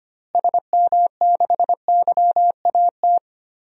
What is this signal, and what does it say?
Morse code, heard as a keyed tone, about 25 wpm, 700 Hz. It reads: SM6YAT
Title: DXCC-100 Call Signs - 6 Characters - Repeated 2x - 25wpm